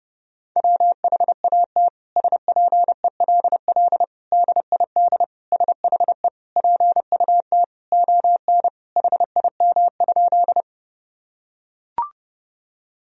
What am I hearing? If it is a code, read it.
Morse code, heard as a keyed tone, about 30 wpm, 700 Hz. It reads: WHAT SPELL DID SHE PUT ON HIM?
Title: Sentences from Top 1000 Words - Review Part 1 - 30wpm